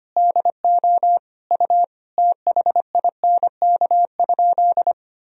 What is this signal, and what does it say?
Morse code, heard as a keyed tone, about 25 wpm, 700 Hz. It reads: DO U THINK?
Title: Sentences from Top 200 Words and Abbreviations 25wpm